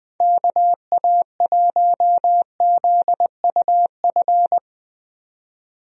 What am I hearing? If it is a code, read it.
Morse code, heard as a keyed tone, about 20 wpm, 700 Hz. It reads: KA1ZUF
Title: USA Call Signs - 6 Characters - 20wpm